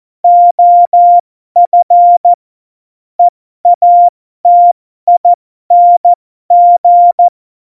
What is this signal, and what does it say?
Morse code, heard as a keyed tone, about 14 wpm, spaced as for 12 wpm, 700 Hz. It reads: OF EATING
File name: Morsekurs / Witze